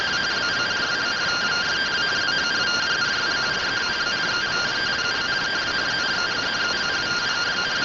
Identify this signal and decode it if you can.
Two-tone teleprinter signal, about 170 Hz shift.
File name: findings / SAGITTAL 43mm 4mm.gen-13 rtty rx